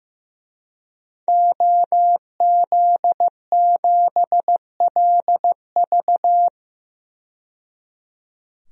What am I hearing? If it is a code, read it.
Morse code, heard as a keyed tone, about 15 wpm, 700 Hz. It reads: OZ7LV
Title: DXCC-100 Call Signs - 5 Characters - 15wpm